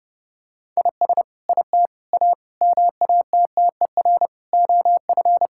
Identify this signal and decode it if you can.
Morse code, heard as a keyed tone, about 28 wpm, 700 Hz. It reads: IS IT A MATTER OF
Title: Sentences from Top 1000 Words - Review Part 5 - 30wpm